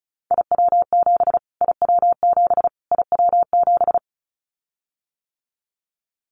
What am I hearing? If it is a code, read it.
Morse code, heard as a keyed tone, about 34 wpm, 700 Hz. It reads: IW7 IW7 IW7
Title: Callsign Prefixes for Top 25 DXCC - Repeated 3x - 35wpm